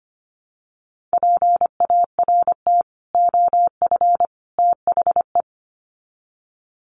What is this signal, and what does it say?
Morse code, heard as a keyed tone, about 25 wpm, 700 Hz. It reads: PART OF THE
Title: Sets of 3 Words 25wpm